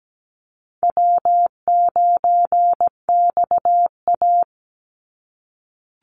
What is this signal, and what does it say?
Morse code, heard as a keyed tone, about 17 wpm, 700 Hz. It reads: W9XA
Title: Call Signs - 17wpm